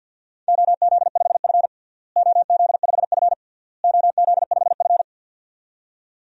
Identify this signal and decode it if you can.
Morse code, heard as a keyed tone, about 45 wpm, 700 Hz. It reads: OZ5F OZ5F OZ5F
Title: DXCC-100 Call Signs - 4 Characters - Repeated 3x - 50wpm